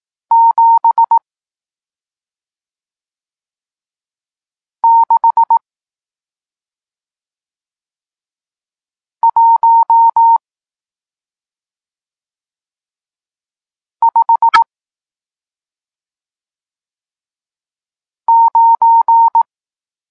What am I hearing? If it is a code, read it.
Morse code, heard as a keyed tone, about 18 wpm, 920 Hz. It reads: 7 6 1 5 9
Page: Tekst 40 – Nauka znaków przestankowych